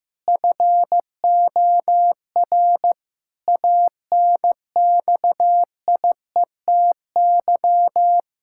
Morse code, heard as a keyed tone, about 15 wpm, 700 Hz. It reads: FOR ANXIETY
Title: Sentences from Top 1600 Words 15wpm